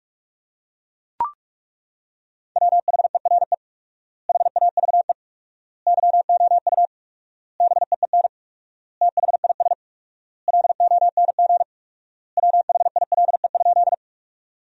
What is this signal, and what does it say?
Morse code, heard as a keyed tone, about 42 wpm, 700 Hz. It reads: WHERE HAVE YOU BEEN THIS LONG WHILE?
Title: 4x Word-Spacing - Sentences from Top 300 Words - 45wpm